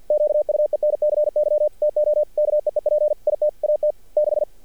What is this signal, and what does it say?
Morse code, heard as a keyed tone, about 50 wpm, 600 Hz. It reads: QUENCY TO KEEP IT AT 6
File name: archived / CRSnet